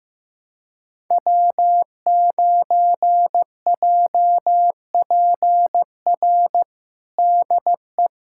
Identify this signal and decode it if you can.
Morse code, heard as a keyed tone, about 16 wpm, 700 Hz. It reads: W9JPR DE
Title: USA Callsign DE Callsign - 5 Characters - 15wpm